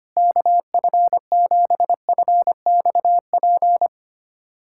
Morse code, heard as a keyed tone, about 25 wpm, 700 Hz. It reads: KF7FXP
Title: USA Callsign DE Callsign - 6 Characters - 25wpm